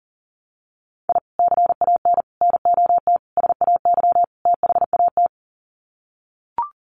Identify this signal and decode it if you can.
Morse code, heard as a keyed tone, about 40 wpm, 700 Hz. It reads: I CAN NOT SAY THAT
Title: Sets of 5 Words Encore2 40wpm